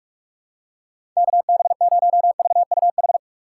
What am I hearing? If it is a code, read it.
Morse code, heard as a keyed tone, about 45 wpm, 700 Hz. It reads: KB0VUH
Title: USA Call Signs - 6 Characters - 2x Character Spacing - 45wpm